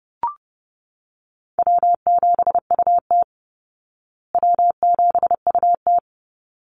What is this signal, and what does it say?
Morse code, heard as a keyed tone, about 30 wpm, 700 Hz. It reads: W7UT W7UT
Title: Call Signs Repeated 3x - 30wpm